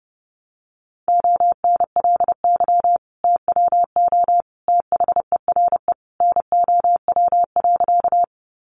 Morse code, heard as a keyed tone, about 30 wpm, 700 Hz. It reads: ONLY TWO THERE NOW.
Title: Sentences from Top 200 Words 30wpm